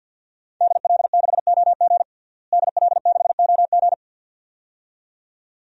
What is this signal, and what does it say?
Morse code, heard as a keyed tone, about 52 wpm, 700 Hz. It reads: DL6OG DL6OG
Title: DXCC-100 Call Signs - 5 Characters - Repeated 2x - 50wpm